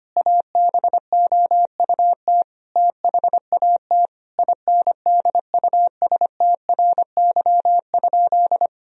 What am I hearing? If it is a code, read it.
Morse code, heard as a keyed tone, about 25 wpm, 700 Hz. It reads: ABOUT THAT INDUSTRY?
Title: Sentences from Top 1000 Words - Review Part 4 - 25wpm